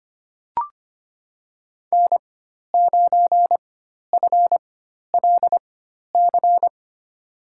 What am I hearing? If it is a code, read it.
Morse code, heard as a keyed tone, about 25 wpm, spaced as for 12 wpm, 700 Hz. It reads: N9FLC
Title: USA Call Signs - 5 Characters - 4x Character Spacing - 25wpm